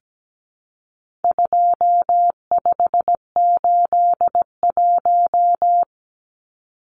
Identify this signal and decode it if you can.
Morse code, heard as a keyed tone, about 17 wpm, 700 Hz. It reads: 2581
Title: Numbers - 4-Digit 17wpm